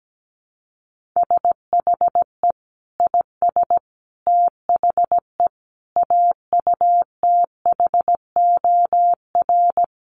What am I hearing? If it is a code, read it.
Morse code, heard as a keyed tone, about 17 wpm, 700 Hz. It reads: SHE IS THE AUTHOR
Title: Sentences from Top 1200 Words 17wpm